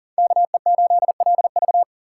Morse code, heard as a keyed tone, about 40 wpm, 700 Hz. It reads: KE8LV
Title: USA Call Signs - 4x Character Spacing - 40wpm